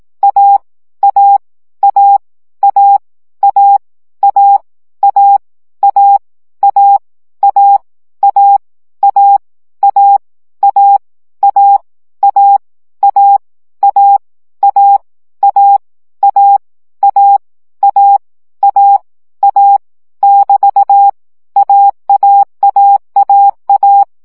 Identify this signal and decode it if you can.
Morse code, heard as a keyed tone, about 18 wpm, 800 Hz. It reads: A A A A A A A A A A A A A A A A A A A A A A A A A <BT> AAAAA
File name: CW gyakorlas a 15 baud